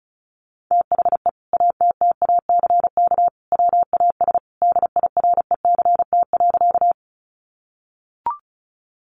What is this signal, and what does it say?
Morse code, heard as a keyed tone, about 34 wpm, 700 Hz. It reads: THE ATTACK WAS DIRECT.
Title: Sentences from Top 1500 Words 35wpm